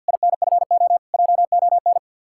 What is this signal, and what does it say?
Morse code, heard as a keyed tone, about 66 wpm, spaced as for 58 wpm, 700 Hz. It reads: INFO JON
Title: Speed-Racing Common QSO Phrases - 50wpm